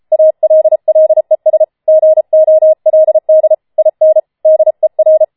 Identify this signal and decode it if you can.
Morse code, heard as a keyed tone, about 33 wpm, 630 Hz. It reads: ALLES GOLD IN DER